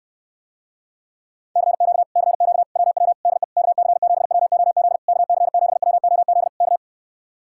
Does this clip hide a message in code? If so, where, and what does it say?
Yes: Morse code from 1.6 to 6.8 s, heard as a keyed tone, about 69 wpm, 700 Hz. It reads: CQ CQ WW DE KC7WOZ KC7WOZ K